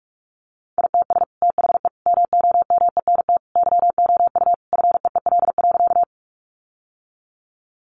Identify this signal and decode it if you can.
Morse code, heard as a keyed tone, about 44 wpm, 700 Hz. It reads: ITS THE MOMENT YOU FEEL.
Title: Sentences from Top 1000 Words 45wpm